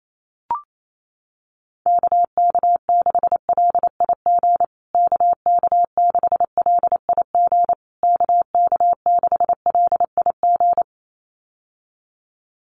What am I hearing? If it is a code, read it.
Morse code, heard as a keyed tone, about 28 wpm, 700 Hz. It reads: KK6LIG KK6LIG KK6LIG
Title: USA Call Signs Repeated 3x 28wpm